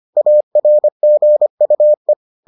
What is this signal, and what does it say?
Morse code, heard as a keyed tone, about 25 wpm, 600 Hz. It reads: ARGUE